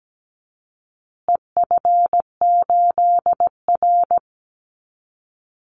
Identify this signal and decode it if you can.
Morse code, heard as a keyed tone, about 17 wpm, 700 Hz. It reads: EF8R
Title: Call Signs Repeated 2x - 17wpm